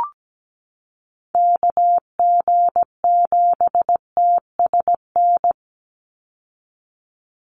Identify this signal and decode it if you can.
Morse code, heard as a keyed tone, about 17 wpm, 700 Hz. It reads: KG7TSN
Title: USA Call Signs 17wpm